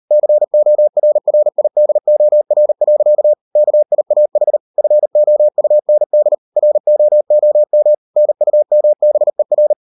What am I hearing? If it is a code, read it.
Morse code, heard as a keyed tone, about 38 wpm, 600 Hz. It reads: CORRIDOR. KIAH FOUND ROOM NUMBER